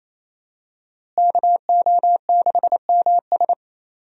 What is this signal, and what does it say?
Morse code, heard as a keyed tone, about 28 wpm, 700 Hz. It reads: KO6MS
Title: USA Call Signs - 5 Characters - Repeated 2x - 28wpm